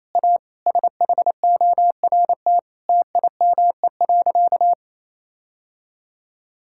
Morse code, heard as a keyed tone, about 28 wpm, 700 Hz. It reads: A SHORT TIME.
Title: Sentences from Top 1600 Words 28wpm